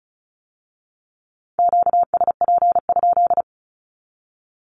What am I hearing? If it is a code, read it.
Morse code, heard as a keyed tone, about 36 wpm, 700 Hz. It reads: QSP?
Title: Q-Codes with Spoken Meaning - 35wpm